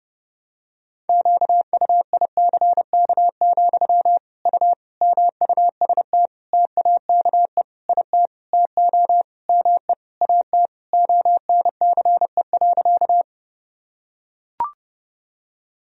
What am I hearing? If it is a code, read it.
Morse code, heard as a keyed tone, about 30 wpm, 700 Hz. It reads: QUICK, U MUST TAKE IT TO ME AT ONCE.
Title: Sentences from Top 500 Words and Abbreviations 30wpm